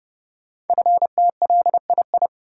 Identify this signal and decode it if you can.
Morse code, heard as a keyed tone, about 30 wpm, 700 Hz. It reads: FTLII
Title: Code Groups - 5 Letters - 30wpm